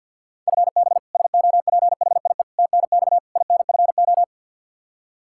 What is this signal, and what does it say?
Morse code, heard as a keyed tone, about 46 wpm, 700 Hz. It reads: FB SOPHIE TNX INFO
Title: Practice QSO - Exchange 3 - More Info - 50wpm